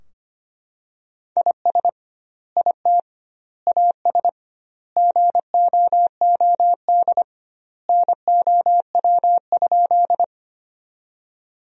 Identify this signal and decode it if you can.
Morse code, heard as a keyed tone, about 25 wpm, 700 Hz. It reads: IS IT AS GOOD NOW?
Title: 2x Word-Spacing - Sentences from Top 200 Words - 25wpm